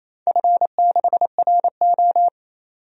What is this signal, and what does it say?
Morse code, heard as a keyed tone, about 28 wpm, 700 Hz. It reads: F6RO
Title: DXCC-100 Call Signs - 4 Characters - 28wpm